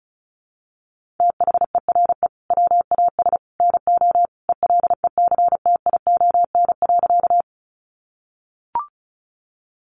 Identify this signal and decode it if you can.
Morse code, heard as a keyed tone, about 34 wpm, 700 Hz. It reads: THERE WAS NO ELECTION.
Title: Sentences from Top 1500 Words 35wpm